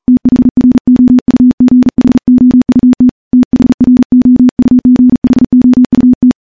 Morse code, heard as a keyed tone, about 40 wpm, 250 Hz. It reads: THROUGHOUT THROUGHOUT
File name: morse / morse-training-40WPM